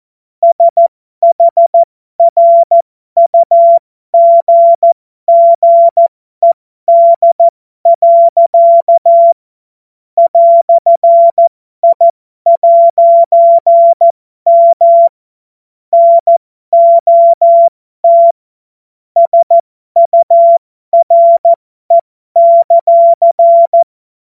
Morse code, heard as a keyed tone, about 14 wpm, spaced as for 12 wpm, 680 Hz. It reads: SHRUGGED. "I'M NOT SURE;
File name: Morsekurs / Witze